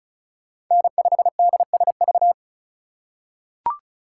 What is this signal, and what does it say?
Morse code, heard as a keyed tone, about 36 wpm, 700 Hz. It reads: N5DSV
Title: USA Call Signs - 5 Characters - 4x Character Spacing - 35wpm